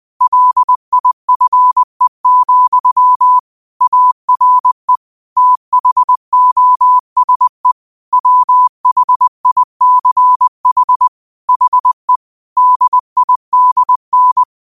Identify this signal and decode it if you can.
Morse code, heard as a keyed tone, about 20 wpm, 1000 Hz. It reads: LIFE, ARE THOSE WHICH HE DIDN